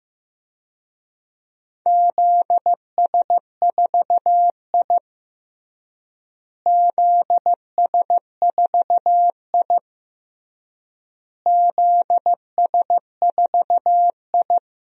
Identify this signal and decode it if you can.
Morse code, heard as a keyed tone, about 15 wpm, 700 Hz. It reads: ZS4I ZS4I ZS4I
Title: DXCC-100 Call Signs - 4 Characters - Repeated 3x - 15wpm